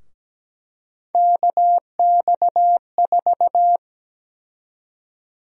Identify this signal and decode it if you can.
Morse code, heard as a keyed tone, about 17 wpm, 700 Hz. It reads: KX4
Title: USA Call Sign Prefixes - 17wpm